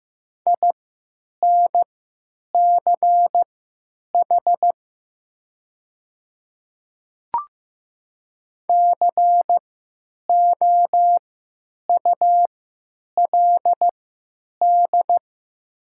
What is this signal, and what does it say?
Morse code, heard as a keyed tone, about 15 wpm, spaced as for 9 wpm, 700 Hz. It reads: INCH COULD
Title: 3x Character Spacing - Top 500 Words - 15wpm